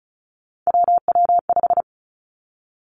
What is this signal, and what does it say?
Morse code, heard as a keyed tone, about 36 wpm, 700 Hz. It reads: WW5
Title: Three Challenging Letter-Number Combinations 35wpm